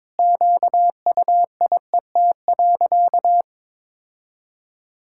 Morse code, heard as a keyed tone, about 21 wpm, 700 Hz. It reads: QUIET.
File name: Sentences with 7-Word Phrases - 22wpm